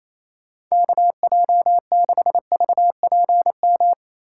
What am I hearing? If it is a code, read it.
Morse code, heard as a keyed tone, about 28 wpm, 700 Hz. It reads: KJ6VPM
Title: USA Call Signs - 6 Characters - 3x Character Spacing - 28wpm